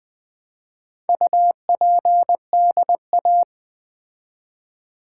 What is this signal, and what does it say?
Morse code, heard as a keyed tone, about 20 wpm, 690 Hz. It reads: UPDA
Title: Build A Word - 2 Letters At A Time - 6-Letter Words - 20wpm